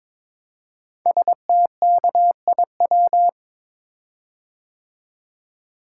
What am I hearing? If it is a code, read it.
Morse code, heard as a keyed tone, about 22 wpm, 700 Hz. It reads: STKIW
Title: Code Groups - 5 Letters - 22wpm